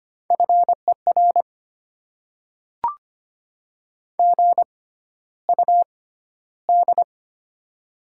Morse code, heard as a keyed tone, about 25 wpm, spaced as for 13 wpm, 700 Hz. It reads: FER GUD
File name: Top 20 QSO Elements - 6x Character Spacing - 25wpm